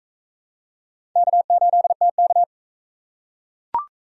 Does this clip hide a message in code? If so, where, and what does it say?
Yes: Morse code from 1.2 to 2.4 s, heard as a keyed tone, about 43 wpm, 700 Hz. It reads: K8TK